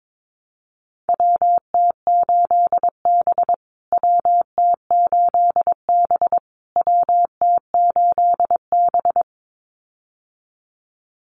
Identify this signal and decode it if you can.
Morse code, heard as a keyed tone, about 22 wpm, 700 Hz. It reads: WT8B WT8B WT8B
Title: USA Call Signs Repeated 3x 22wpm